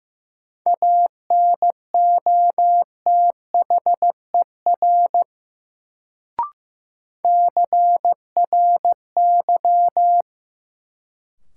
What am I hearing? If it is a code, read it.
Morse code, heard as a keyed tone, about 15 wpm, 700 Hz. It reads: ANOTHER CRY
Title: Sets of 5 Words Encore - New Word Drill - 15wpm